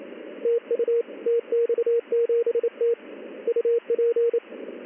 Morse code, heard as a keyed tone, about 28 wpm, 470 Hz, with radio noise. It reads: TU TX7T UP
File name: contests / ss-ssb-k5tr- -1700